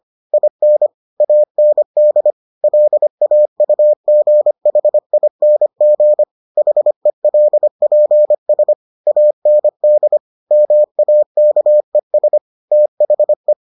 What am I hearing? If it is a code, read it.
Morse code, heard as a keyed tone, about 25 wpm, 600 Hz. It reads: IN AND LAUGHING HELPS AND MAKES THE